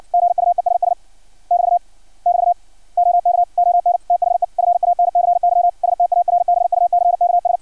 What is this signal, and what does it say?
Morse code, heard as a keyed tone, about 56 wpm, 700 Hz. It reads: GUESS <BT> <BT> OK ON THE FANCY STARBUCKS
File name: archived / CRSnet